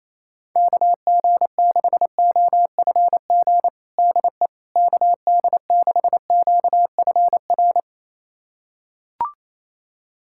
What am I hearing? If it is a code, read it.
Morse code, heard as a keyed tone, about 28 wpm, 700 Hz. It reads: KG6OFG DE KD6QFR
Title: USA Callsign DE Callsign 28wpm